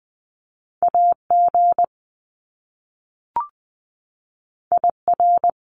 Morse code, heard as a keyed tone, about 20 wpm, 700 Hz. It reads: AG IR
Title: Top Two Letter Combinations in Words 20wpm